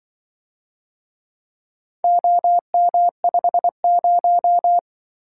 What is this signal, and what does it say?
Morse code, heard as a keyed tone, about 25 wpm, 700 Hz. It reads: OM50